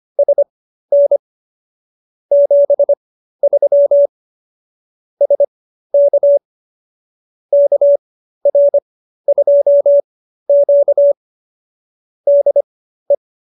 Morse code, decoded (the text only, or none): SN 73 SK KR2Q DE